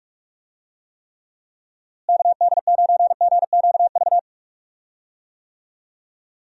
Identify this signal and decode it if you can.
Morse code, heard as a keyed tone, about 42 wpm, 700 Hz. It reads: KD9GQV
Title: USA Call Signs - 6 Characters - Repeated 3x - 45wpm